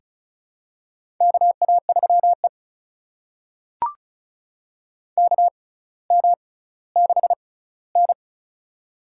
Morse code, heard as a keyed tone, about 36 wpm, spaced as for 18 wpm, 700 Hz. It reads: KA3E KM6N
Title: USA Call Signs - 6x Character Spacing - 35wpm